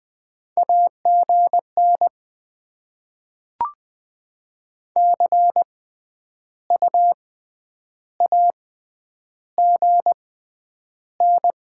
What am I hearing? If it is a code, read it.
Morse code, heard as a keyed tone, about 20 wpm, 700 Hz. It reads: A G N C U A G N
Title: Top 20 QSO Elements - 6x Character Spacing - 20wpm